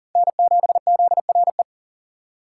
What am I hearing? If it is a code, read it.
Morse code, heard as a keyed tone, about 40 wpm, 700 Hz. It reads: N7ZRE
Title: USA Call Signs - 5 Characters - 3x Character Spacing - 40wpm